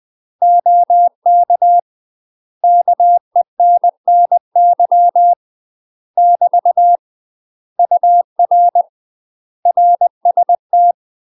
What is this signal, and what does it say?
Morse code, heard as a keyed tone, about 20 wpm, 700 Hz. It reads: OK KENNY <BT> UR RST
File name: UK MASTER QSOs TRAINING / 20WPM / UK MASTER QSOs 0046